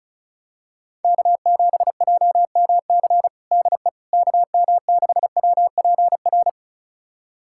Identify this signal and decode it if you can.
Morse code, heard as a keyed tone, about 36 wpm, 700 Hz. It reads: K7JMC DE KM6WPR
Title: USA Callsign DE Callsign 35wpm